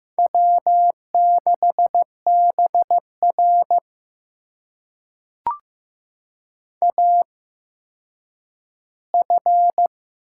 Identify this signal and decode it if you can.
Morse code, heard as a keyed tone, about 15 wpm, spaced as for 9 wpm, 700 Hz. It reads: W6BR AF
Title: USA Call Signs - 4 Characters - 8x Character Spacing - 15wpm